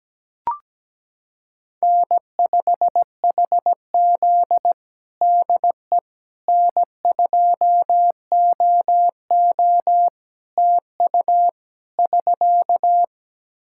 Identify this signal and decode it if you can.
Morse code, heard as a keyed tone, about 17 wpm, 700 Hz. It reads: N5HZ DE N2OO TU <SK>
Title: Practice QSO - Exchange 4 - Ending - 17wpm